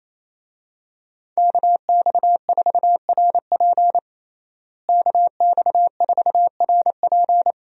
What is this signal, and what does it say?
Morse code, heard as a keyed tone, about 28 wpm, 700 Hz. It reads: KX4RP KX4RP
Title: USA Call Signs - 5 Characters - Repeated 2x - 28wpm